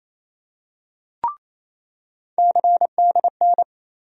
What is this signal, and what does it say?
Morse code, heard as a keyed tone, about 28 wpm, 700 Hz. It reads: CDN
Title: USA Call Sign Postfixes - 3 Letters 28wpm